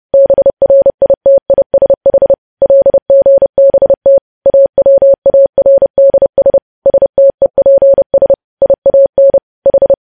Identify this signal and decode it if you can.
Morse code, heard as a keyed tone, about 30 wpm, 550 Hz. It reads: BRITISH LGBT AWARDS STEPS IAN H
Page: news headlines in morse code at 30 wpm